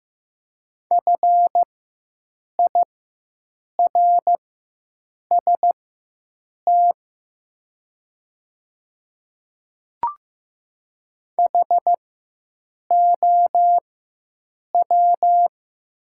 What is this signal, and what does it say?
Morse code, heard as a keyed tone, about 15 wpm, spaced as for 7 wpm, 700 Hz. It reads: FIRST HOW
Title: 4x Character Spacing - Top 100 Words - 15wpm